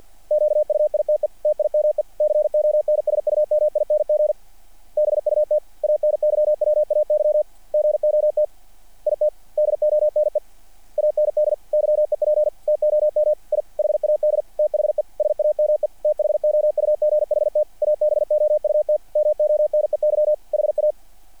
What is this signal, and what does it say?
Morse code, heard as a keyed tone, about 48 wpm, 600 Hz. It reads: QUITE TIME CONSUMING BUT ANYWAY GOT IT DONE AND YEP TOM I HAD THE SAME THOUGHT ABOUT MONEY HA
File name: archived / CRSnet